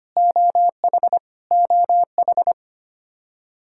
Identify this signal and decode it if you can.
Morse code, heard as a keyed tone, about 25 wpm, 700 Hz. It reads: OH OH
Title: Practice QSO - Exchange 1 - Beginning - 25wpm